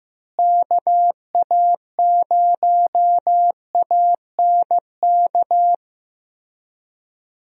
Morse code, heard as a keyed tone, about 15 wpm, 700 Hz. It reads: KA0ANK
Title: USA Call Signs - 6 Characters - Repeated 3x - 15wpm